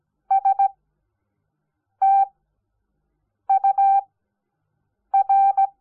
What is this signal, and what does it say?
Morse code, heard as a keyed tone, about 17 wpm, spaced as for 6 wpm, 780 Hz. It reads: STUR
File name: Onlinelehrgang / morsen / DJ4UF Morsen05